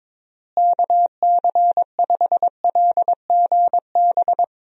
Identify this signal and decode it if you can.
Morse code, heard as a keyed tone, about 22 wpm, 700 Hz. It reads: KC5LGB
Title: USA Call Signs - 6 Characters - Repeated 3x - 22wpm